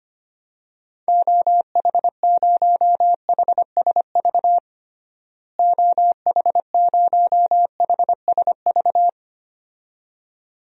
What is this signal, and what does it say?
Morse code, heard as a keyed tone, about 25 wpm, 700 Hz. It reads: OH0HSV OH0HSV
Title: DXCC-100 Call Signs - 6 Characters - Repeated 2x - 25wpm